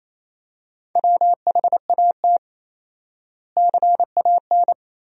Morse code, heard as a keyed tone, about 28 wpm, 700 Hz. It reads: WHAT CAN
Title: 4x Word-Spacing - Sentences from Top 400 Words - 28wpm